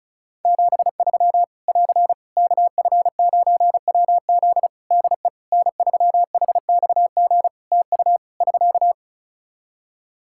Morse code, heard as a keyed tone, about 36 wpm, 700 Hz. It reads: 73 <AR> KF9WZ DE N3HXG TU <SK>
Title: Practice QSO - Exchange 4 - Ending - 35wpm